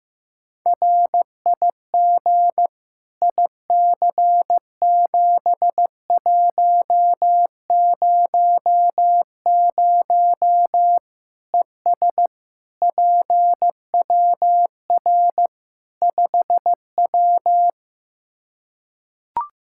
Morse code, heard as a keyed tone, about 15 wpm, 700 Hz. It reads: RIG IC7100 ES PWR 5W
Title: Practice QSO - Exchange 2 - Info - 15wpm